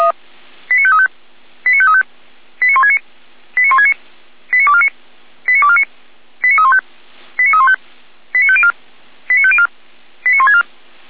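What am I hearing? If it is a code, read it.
Five-tone selective call (ZVEI), heard as quick runs of five tones, tones of about 70 ms. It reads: 89535 89535 89169 89169 89249 89249 89315 89315 89504 89504 89164
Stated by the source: ZVEI five-tone call 89250 received by SDR.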